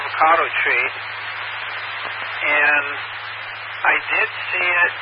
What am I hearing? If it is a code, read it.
Emergency Alert System (SAME) tones, heard as harsh buzzing bursts, with speech. It is heard from 1.0 to 4.1 s.